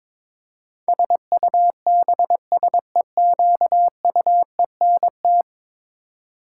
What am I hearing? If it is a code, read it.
Morse code, heard as a keyed tone, about 22 wpm, 700 Hz. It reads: SUBSEQUENT
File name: practice / 22wpm / 10 Letter Words - First 3 Letters Given - 22wpm